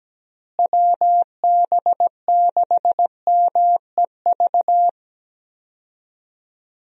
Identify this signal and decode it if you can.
Morse code, heard as a keyed tone, about 17 wpm, 700 Hz. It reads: WB6MEV
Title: USA Call Signs - 6 Characters - Repeated 3x - 17wpm